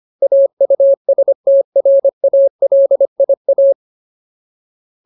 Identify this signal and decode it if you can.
Morse code, heard as a keyed tone, about 25 wpm, 550 Hz. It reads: AUSTRALIA
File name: Ultimate Word List / 25WPM Ultimate